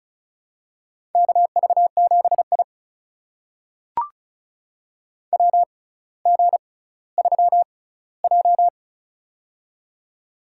Morse code, heard as a keyed tone, about 36 wpm, spaced as for 18 wpm, 700 Hz. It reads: KV7I WG3J
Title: USA Call Signs - 4 Characters - 6x Character Spacing - 35wpm